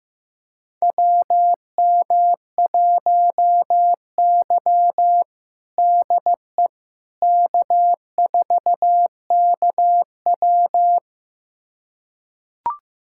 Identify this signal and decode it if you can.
Morse code, heard as a keyed tone, about 15 wpm, 700 Hz. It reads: WM1Y DE K4KW
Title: USA Callsign DE Callsign - 4 Characters - 15wpm